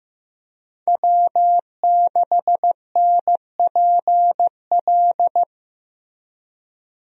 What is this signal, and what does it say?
Morse code, heard as a keyed tone, about 16 wpm, 700 Hz. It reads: W6NPL
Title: USA Call Signs - 5 Characters - Repeated 3x - 15wpm